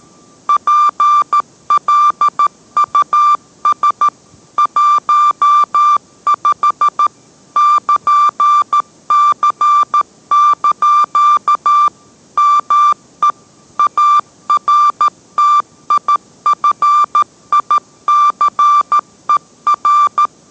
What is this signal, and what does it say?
Morse code, heard as a keyed tone, about 13 wpm, 1180 Hz. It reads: PLUS 15 <KN>C) ME ARTIFICER